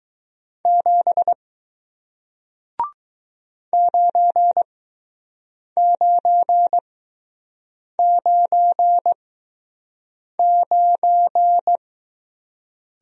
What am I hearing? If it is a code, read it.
Morse code, heard as a keyed tone, about 20 wpm, spaced as for 6 wpm, 700 Hz. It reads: 7 9999